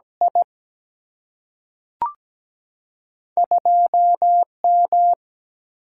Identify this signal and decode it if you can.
Morse code, heard as a keyed tone, about 17 wpm, 700 Hz. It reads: I 2M
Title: Single-Letter Single-Number 17wpm